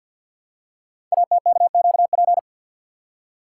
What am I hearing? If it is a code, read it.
Morse code, heard as a keyed tone, about 50 wpm, 700 Hz. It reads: ATKQP
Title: Code Groups - 5 Letters - 50wpm